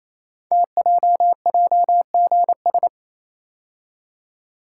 Morse code, heard as a keyed tone, about 29 wpm, 700 Hz. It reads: TJJGS
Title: Code Groups - 5 Letters - 28wpm